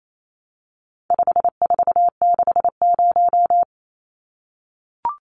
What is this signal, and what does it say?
Morse code, heard as a keyed tone, about 28 wpm, 700 Hz. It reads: 5460
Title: Numbers - 4-Digit 28wpm